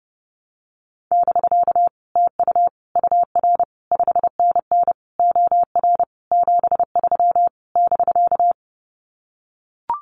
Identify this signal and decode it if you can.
Morse code, heard as a keyed tone, about 30 wpm, 700 Hz. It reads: <BK> TU UR 5NN OR 73 <BK>
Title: Common QSO Phrases Encore 30wpm